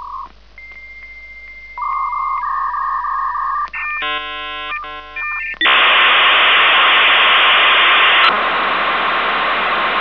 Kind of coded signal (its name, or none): dial-up modem handshake